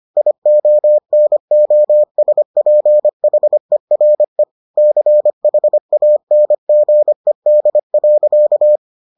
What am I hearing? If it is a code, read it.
Morse code, heard as a keyed tone, about 25 wpm, 600 Hz. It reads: IONOSPHERE CHANGED.